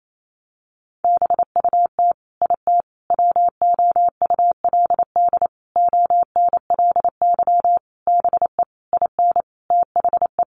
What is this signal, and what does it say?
Morse code, heard as a keyed tone, about 28 wpm, 700 Hz. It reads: BUT IT WOULD ONLY BE IN THE
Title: Sentences from Top 1500 Words 28wpm